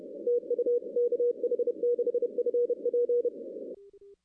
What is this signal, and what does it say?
Morse code, heard as a keyed tone, about 30 wpm, 480 Hz. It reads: TU KH6FP
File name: 2016 arrl dx cw / 2016-ARRL-DX-CW-K5TR- - 0-25